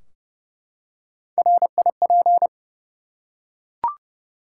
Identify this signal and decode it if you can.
Morse code, heard as a keyed tone, about 31 wpm, 700 Hz. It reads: RIP
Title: Three Letter Words - 30wpm